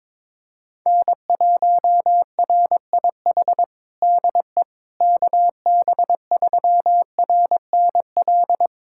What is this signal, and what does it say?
Morse code, heard as a keyed tone, about 22 wpm, 700 Hz. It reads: N1RIH DE KB3RNL
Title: USA Callsign DE Callsign 22wpm